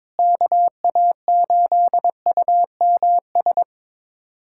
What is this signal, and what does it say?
Morse code, heard as a keyed tone, about 22 wpm, 700 Hz. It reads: KA8UMS
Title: USA Callsign DE Callsign - 6 Characters - 22wpm